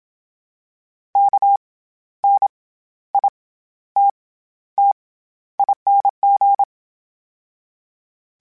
Morse code, heard as a keyed tone, about 27 wpm, 800 Hz. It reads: K N I T T ING
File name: Special INGs-mp3s / Special INGs-mp3s / SlowToFaster ing @25